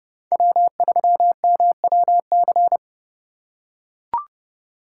Morse code, heard as a keyed tone, about 29 wpm, 700 Hz. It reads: W3MWC